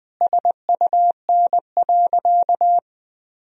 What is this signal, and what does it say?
Morse code, heard as a keyed tone, about 20 wpm, 700 Hz. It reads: SUN.
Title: Sentences from Top 1000 Words - Review Part 2 - 20wpm